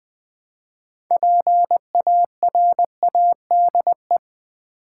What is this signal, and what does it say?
Morse code, heard as a keyed tone, about 20 wpm, 700 Hz. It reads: PARADE
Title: Build A Word - 3 Letters At A Time - 6-Letter Words - 20wpm